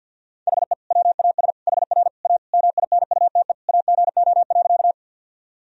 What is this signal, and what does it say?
Morse code, heard as a keyed tone, about 52 wpm, 700 Hz. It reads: HE WAS HR A MINUTE AGO.
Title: Sentences from Top 500 Words and Abbreviations 50wpm